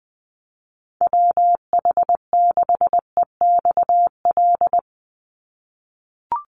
Morse code, heard as a keyed tone, about 20 wpm, 700 Hz. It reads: WH6EXL
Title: USA Call Signs 20wpm